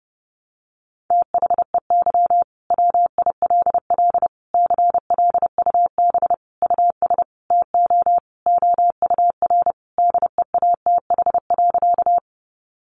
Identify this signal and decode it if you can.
Morse code, heard as a keyed tone, about 30 wpm, 700 Hz. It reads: THEY WILL CLUB US TO OUR DEATH.
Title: Sentences from Top 1200 Words 30wpm